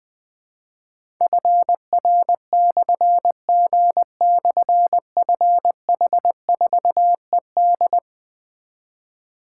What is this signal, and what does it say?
Morse code, heard as a keyed tone, about 21 wpm, 700 Hz. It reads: FR/G/FH4ED
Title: Difficult Call Signs 20wpm